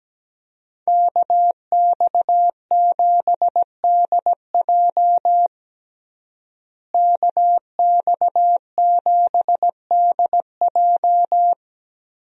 Morse code, heard as a keyed tone, about 17 wpm, 700 Hz. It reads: KX7DJ KX7DJ
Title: USA Call Signs - 5 Characters - Repeated 3x - 17wpm